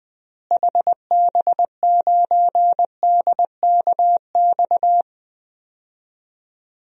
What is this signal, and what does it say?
Morse code, heard as a keyed tone, about 20 wpm, 700 Hz. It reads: HB9DKX
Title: DXCC-100 Call Signs - 6 Characters - Repeated 3x - 20wpm